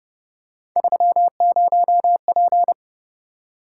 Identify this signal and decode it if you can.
Morse code, heard as a keyed tone, about 31 wpm, 700 Hz. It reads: 30P
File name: Speed-Racing - Three Letter-Number Combinations - 25wpm